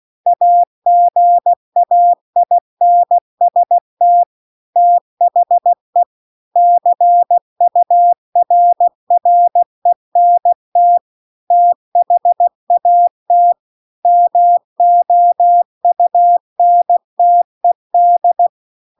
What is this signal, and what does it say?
Morse code, heard as a keyed tone, about 16 wpm, 700 Hz. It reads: AGAINST THE CURRENT THAT MOUNTED
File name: CW-The-Big-Twoheaded-River-2-16WPM